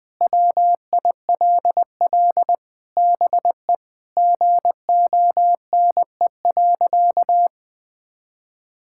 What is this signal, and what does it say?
Morse code, heard as a keyed tone, about 21 wpm, 700 Hz. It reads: WILL BE GONE.
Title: Sentences with 4-Word Phrases - 20wpm